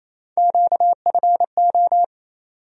Morse code, heard as a keyed tone, about 28 wpm, 700 Hz. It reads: QFO
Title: USA Call Sign Postfixes - 3 Letters 28wpm